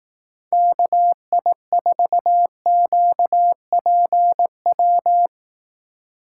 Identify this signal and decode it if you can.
Morse code, heard as a keyed tone, about 18 wpm, 700 Hz. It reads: KI4QPW
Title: Speed Racing - USA Call Signs 15wpm